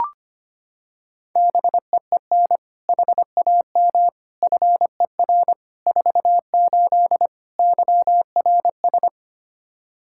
Morse code, heard as a keyed tone, about 25 wpm, 700 Hz. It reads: BEEN HAM FER 48 YRS
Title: Practice QSO - Exchange 3 - More Info - 25wpm